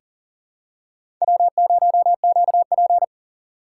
Morse code, heard as a keyed tone, about 40 wpm, 700 Hz. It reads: W0QP
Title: USA Call Signs - 4 Characters - 4x Character Spacing - 40wpm